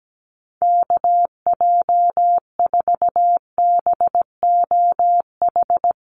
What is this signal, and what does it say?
Morse code, heard as a keyed tone, about 17 wpm, 700 Hz. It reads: KJ4BOH
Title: USA Call Signs Repeated 2x 17wpm